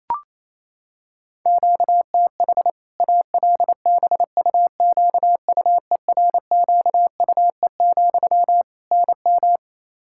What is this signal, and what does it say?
Morse code, heard as a keyed tone, about 27 wpm, 700 Hz. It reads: QTH ALBUQUERQUE, NM
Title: Practice QSO - Exchange 1 - Beginning - 28wpm